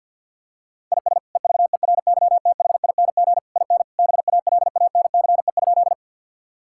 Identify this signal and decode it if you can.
Morse code, heard as a keyed tone, about 50 wpm, 700 Hz. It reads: IS EVERYTHING IN BALANCE?
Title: Sentences from Top 1000 Words - Review Part 6 - 50wpm